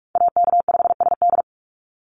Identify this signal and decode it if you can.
Morse code, heard as a keyed tone, about 46 wpm, 700 Hz. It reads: AK5SD
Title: NA SOTA Chaser Callsigns 45wpm